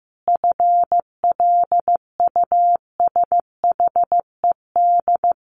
Morse code, heard as a keyed tone, about 15 wpm, 700 Hz. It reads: FLUSHED
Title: A Sound Of Thunder 15wpm